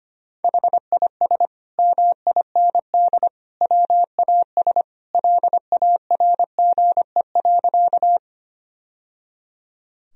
Morse code, heard as a keyed tone, about 25 wpm, 700 Hz. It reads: HIS MIND WAS LARGE.
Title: Sentences from Top 1000 Words - Review Part 1 - 25wpm